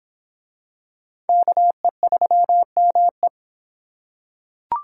Morse code, heard as a keyed tone, about 26 wpm, 700 Hz. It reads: KE3ME